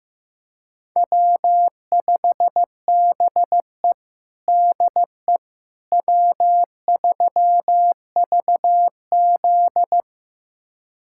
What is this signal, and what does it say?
Morse code, heard as a keyed tone, about 15 wpm, 700 Hz. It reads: W5BE DE W3VZ
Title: USA Callsign DE Callsign - 4 Characters - 15wpm